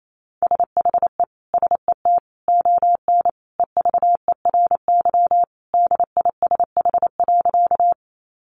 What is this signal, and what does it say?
Morse code, heard as a keyed tone, about 28 wpm, 700 Hz. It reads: SHE SET ON EVERY DISH.
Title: Sentences from Top 1300 Words 28wpm